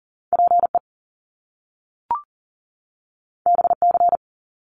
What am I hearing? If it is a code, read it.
Morse code, heard as a keyed tone, about 41 wpm, 700 Hz. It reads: PE BC
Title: Canadian Provinces and Territories 40wpm